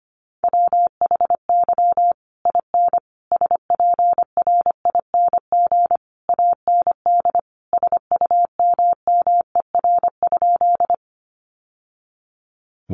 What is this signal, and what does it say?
Morse code, heard as a keyed tone, about 25 wpm, 700 Hz. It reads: WHY IN SPRING AND SUMMER?
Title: Sentences from Top 1200 Words 25wpm